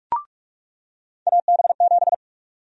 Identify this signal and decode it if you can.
Morse code, heard as a keyed tone, about 46 wpm, 700 Hz. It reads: AB7
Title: USA Call Sign Prefixes - 45wpm